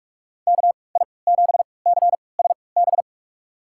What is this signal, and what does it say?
Morse code, heard as a keyed tone, about 41 wpm, spaced as for 25 wpm, 690 Hz. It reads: KI7CSB
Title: USA Call Signs - 6 Characters - 3x Character Spacing - 45wpm